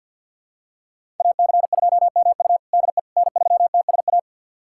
Morse code, heard as a keyed tone, about 50 wpm, 690 Hz. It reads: AC2MU DE N3TSA
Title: USA Callsign DE Callsign - 5 Characters - 50wpm